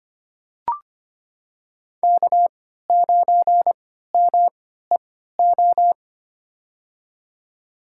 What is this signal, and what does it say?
Morse code, heard as a keyed tone, about 25 wpm, spaced as for 14 wpm, 700 Hz. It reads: K9MEO
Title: USA Call Signs - 3x Character Spacing - 25wpm